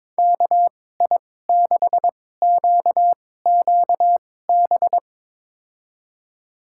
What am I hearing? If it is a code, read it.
Morse code, heard as a keyed tone, about 22 wpm, spaced as for 16 wpm, 700 Hz. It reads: KI6QQB
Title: USA Call Signs - 6 Characters - 2x Character Spacing - 22wpm